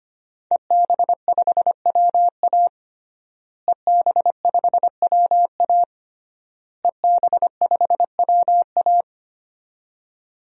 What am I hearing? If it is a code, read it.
Morse code, heard as a keyed tone, about 25 wpm, 700 Hz. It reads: EB5WA EB5WA EB5WA
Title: DXCC-100 Call Signs - 5 Characters - Repeated 3x - 25wpm